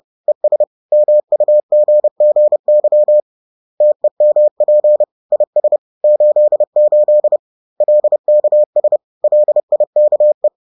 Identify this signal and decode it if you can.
Morse code, heard as a keyed tone, about 30 wpm, 600 Hz. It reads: ES MUGGY TEMP IS 88 LKS LIKE
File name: 30wpm-4